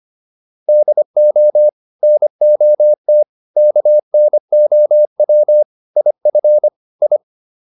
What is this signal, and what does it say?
Morse code, heard as a keyed tone, about 25 wpm, 600 Hz. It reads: DO NOT KNOW IF I